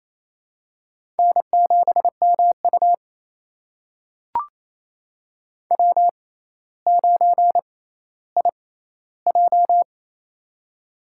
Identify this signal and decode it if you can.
Morse code, heard as a keyed tone, about 27 wpm, spaced as for 14 wpm, 700 Hz. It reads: N7MU W9IJ
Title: USA Call Signs - 4 Characters - 6x Character Spacing - 28wpm